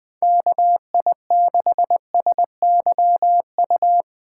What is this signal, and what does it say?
Morse code, heard as a keyed tone, about 20 wpm, 700 Hz. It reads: KI6SYU